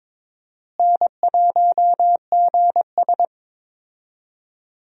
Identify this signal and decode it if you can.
Morse code, heard as a keyed tone, about 21 wpm, 700 Hz. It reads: N1GS
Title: USA Call Signs - 4 Characters - Repeated 2x - 22wpm